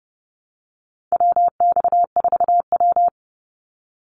Morse code, heard as a keyed tone, about 30 wpm, 700 Hz. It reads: WX4W
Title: Call Signs Repeated 3x - 30wpm